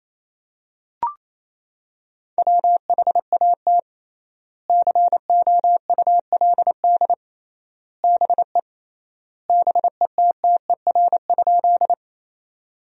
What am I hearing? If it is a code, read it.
Morse code, heard as a keyed tone, about 28 wpm, 700 Hz. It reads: WHAT COULD BE BETTER?
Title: 3x Word-Spacing - Sentences from Top 400 Words - 28wpm